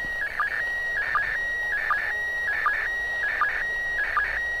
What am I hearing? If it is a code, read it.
SSTV (slow-scan television), PD 180.
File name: ISS 2015-04-11 145 800MHz SSTV